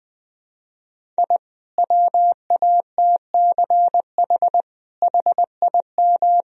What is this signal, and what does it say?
Morse code, heard as a keyed tone, about 21 wpm, 700 Hz. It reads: I WATCH HIM
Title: Sentences from Top 1000 Words - Review Part 1 - 20wpm